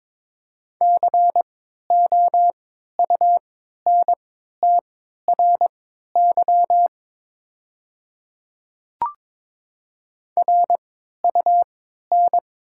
Morse code, heard as a keyed tone, about 22 wpm, spaced as for 12 wpm, 700 Hz. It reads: COUNTRY RUN
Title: 3x Character Spacing - Top 300 Words - 22wpm